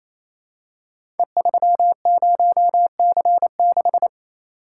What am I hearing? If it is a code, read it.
Morse code, heard as a keyed tone, about 27 wpm, 700 Hz. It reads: E30C6
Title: Code Groups - 5 Letter:Numbers - 28wpm